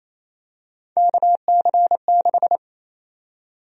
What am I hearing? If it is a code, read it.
Morse code, heard as a keyed tone, about 28 wpm, 700 Hz. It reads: KC6
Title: USA Call Sign Prefixes - 28wpm